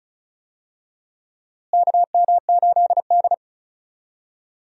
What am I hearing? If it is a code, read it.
Morse code, heard as a keyed tone, about 36 wpm, 700 Hz. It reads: KM8D
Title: USA Call Signs - 4 Characters - Repeated 2x - 35wpm